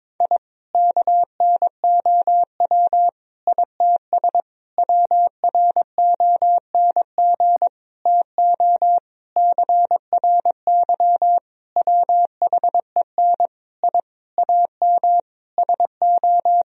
Morse code, heard as a keyed tone, about 22 wpm, 700 Hz. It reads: I KNOW ITS WRONG TO CRY WHEN I AM SO
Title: Sentences from Top 1000 Words - Review Part 2 - 22wpm